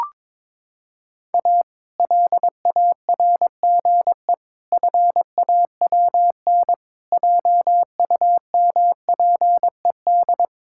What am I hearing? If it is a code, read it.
Morse code, heard as a keyed tone, about 22 wpm, 700 Hz. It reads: A LARGE FAWN JUMPED
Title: Pangram Challenge v2 - 22wpm